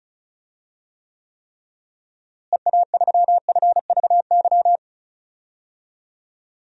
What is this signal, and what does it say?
Morse code, heard as a keyed tone, about 34 wpm, 700 Hz. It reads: EA3FVY
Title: DXCC-100 Call Signs - 6 Characters - Repeated 3x - 35wpm